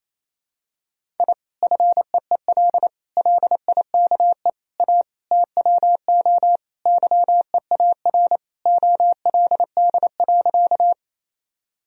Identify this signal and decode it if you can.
Morse code, heard as a keyed tone, about 28 wpm, 700 Hz. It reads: I FEEL LIKE A TWO YEAR OLD.
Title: Sentences from Top 1000 Words - Review Part 1 - 28wpm